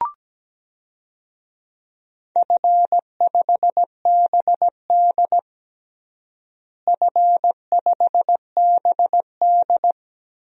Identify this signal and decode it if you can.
Morse code, heard as a keyed tone, about 17 wpm, 700 Hz. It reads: F5BD F5BD
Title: DXCC-100 Call Signs - 4 Characters - Repeated 3x - 17wpm